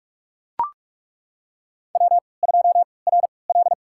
Morse code, heard as a keyed tone, about 41 wpm, spaced as for 25 wpm, 700 Hz. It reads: W2RL
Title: USA Call Signs - 3x Character Spacing - 45wpm